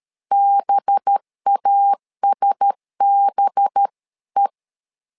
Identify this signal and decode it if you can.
Morse code, heard as a keyed tone, about 13 wpm, 800 Hz. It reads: BASBE
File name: LEZIONI / 5 LBVC / 5 LB 12wpm 1906